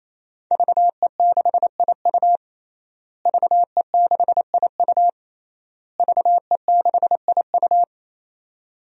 Morse code, heard as a keyed tone, about 28 wpm, 700 Hz. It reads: VE6IU VE6IU VE6IU
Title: DXCC-100 Call Signs - 5 Characters - Repeated 3x - 28wpm